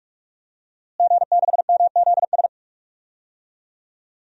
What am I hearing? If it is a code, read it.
Morse code, heard as a keyed tone, about 48 wpm, 700 Hz. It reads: G6MZS
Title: DXCC-100 Call Signs - 5 Characters - Repeated 2x - 45wpm